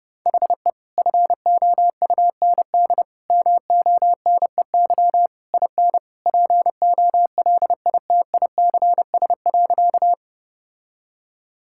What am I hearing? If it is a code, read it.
Morse code, heard as a keyed tone, about 30 wpm, 700 Hz. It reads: HE FOUND MONEY IN POLITICS.
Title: Sentences from Top 1000 Words - Review Part 6 - 30wpm